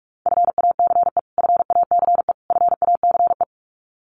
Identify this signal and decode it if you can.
Morse code, heard as a keyed tone, about 46 wpm, 700 Hz. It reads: FACE FACE FACE
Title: Top 300 Words Repeated 45wpm